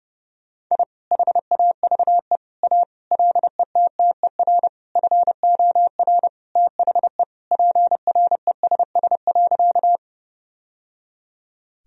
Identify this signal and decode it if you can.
Morse code, heard as a keyed tone, about 32 wpm, 700 Hz. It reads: I HAVE A LETTER FOR THE PRESS.
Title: Sentences from Top 1000 Words - Review Part 1 - 30wpm